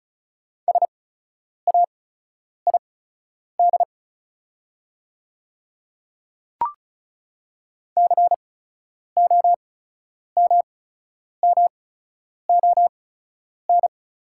Morse code, heard as a keyed tone, about 36 wpm, spaced as for 10 wpm, 700 Hz. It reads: SAID COMMON
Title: 8x Character Spacing - Top 500 Words - 35wpm